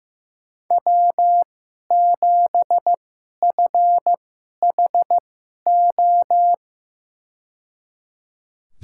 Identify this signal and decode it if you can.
Morse code, heard as a keyed tone, about 15 wpm, spaced as for 11 wpm, 700 Hz. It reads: W7FHO
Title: USA Call Signs - 5 Characters - 2x Character Spacing - 15wpm